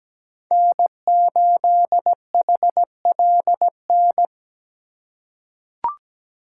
Morse code, heard as a keyed tone, about 17 wpm, 700 Hz. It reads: N8HLN
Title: USA Call Signs - 5 Characters - 2x Character Spacing - 17wpm